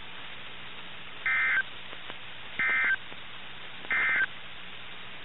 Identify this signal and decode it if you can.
Emergency Alert System (SAME) tones, heard as harsh buzzing bursts. It is heard from 1.1 to 4.3 s.
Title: NOAA Weather Radio - KIH61